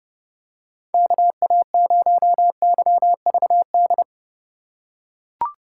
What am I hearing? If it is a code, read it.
Morse code, heard as a keyed tone, about 32 wpm, 700 Hz. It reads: KA0YVD
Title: USA Call Signs - 6 Characters - 2x Character Spacing - 30wpm